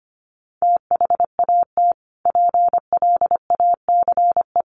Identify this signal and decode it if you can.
Morse code, heard as a keyed tone, about 25 wpm, 700 Hz. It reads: THAT PLACE
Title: Sentences from Top 1500 Words 25wpm